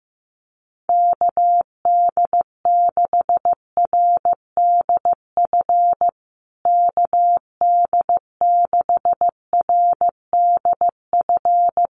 Morse code, heard as a keyed tone, about 15 wpm, 700 Hz. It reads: KD6RDF KD6RDF
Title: USA Call Signs Repeated 3x 15wpm